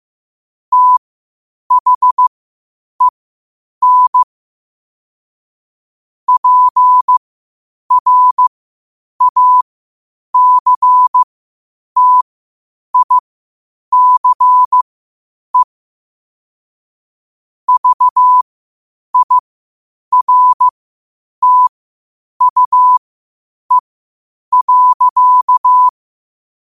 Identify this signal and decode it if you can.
Morse code, heard as a keyed tone, about 15 wpm, spaced as for 8 wpm, 1000 Hz. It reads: THEN PRACTICE VIRTUE.
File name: qotdfiles / 8wpm